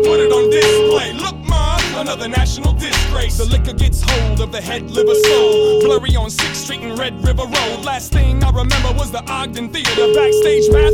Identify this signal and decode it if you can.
Ringback tone (Europe, 425 Hz), with noise.